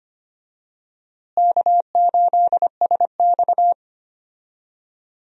Morse code, heard as a keyed tone, about 25 wpm, 700 Hz. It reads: K8SX
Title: USA Call Signs - 4 Characters - Repeated 3x - 25wpm